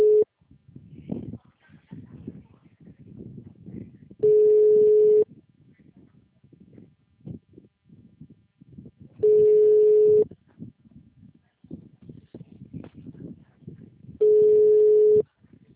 Ringback tone (Europe, 425 Hz).
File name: out- -2010- - - .28361